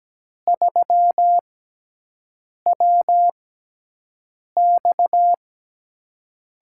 Morse code, heard as a keyed tone, about 17 wpm, spaced as for 6 wpm, 700 Hz. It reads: 3WX
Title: USA Call Signs - 5 Characters - 6x Character Spacing - 17wpm